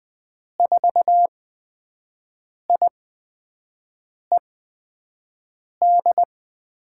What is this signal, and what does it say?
Morse code, heard as a keyed tone, about 20 wpm, spaced as for 5 wpm, 700 Hz. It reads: 4IED
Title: USA Call Signs - 8x Character Spacing - 20wpm